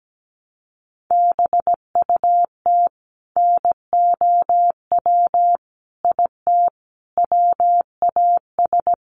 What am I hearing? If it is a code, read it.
Morse code, heard as a keyed tone, about 17 wpm, 700 Hz. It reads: BUT NOW IT WAS
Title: Sentences from Top 700 Words 17wpm